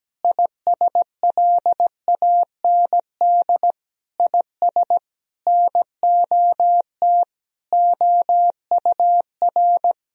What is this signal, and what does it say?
Morse code, heard as a keyed tone, about 17 wpm, 700 Hz. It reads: ISLAND IS NOT OUR
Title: Sentences from Top 1000 Words - Review Part 4 - 17wpm